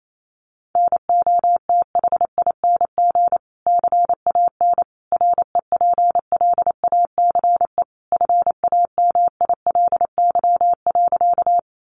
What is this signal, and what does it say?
Morse code, heard as a keyed tone, about 28 wpm, 700 Hz. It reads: NOTHING CAN REPLACE FAMILY.
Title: Sentences from Top 1400 Words 28wpm